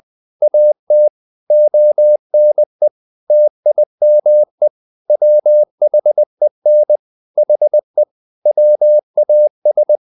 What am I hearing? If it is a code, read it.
Morse code, heard as a keyed tone, about 20 wpm, 600 Hz. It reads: AT ONE TIME WHEN HE WAS
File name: the old man and the sea part 2 20wpm